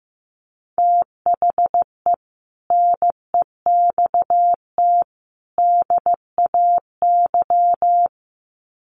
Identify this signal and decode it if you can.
Morse code, heard as a keyed tone, about 15 wpm, 700 Hz. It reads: THE NEXT DAY
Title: Sets of 3 Words 15wpm